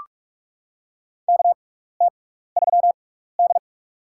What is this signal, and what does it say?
Morse code, heard as a keyed tone, about 46 wpm, spaced as for 16 wpm, 700 Hz. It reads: KT3D
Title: USA Call Signs - 4 Characters - 6x Character Spacing - 45wpm